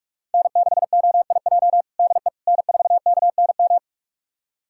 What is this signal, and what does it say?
Morse code, heard as a keyed tone, about 46 wpm, 700 Hz. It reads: N6OIJ DE N4KNM
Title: USA Callsign DE Callsign - 5 Characters - 45wpm